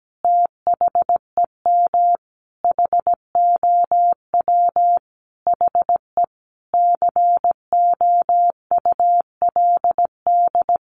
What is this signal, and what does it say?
Morse code, heard as a keyed tone, about 17 wpm, 700 Hz. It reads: THEM HOW HE COULD
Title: Sentences from Top 1300 Words 17wpm